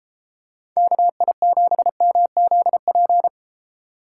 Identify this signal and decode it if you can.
Morse code, heard as a keyed tone, about 33 wpm, 700 Hz. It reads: KI7MZP